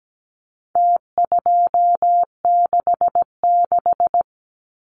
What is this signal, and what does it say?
Morse code, heard as a keyed tone, about 17 wpm, 700 Hz. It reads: T266
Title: Numbers - Contest Style - 4-Digit 17wpm